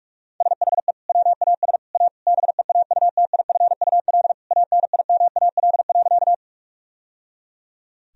Morse code, heard as a keyed tone, about 46 wpm, 700 Hz. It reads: SHE WAS A BEAUTIFUL ANIMAL.
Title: Sentences from Top 1000 Words - Review Part 1 - 45wpm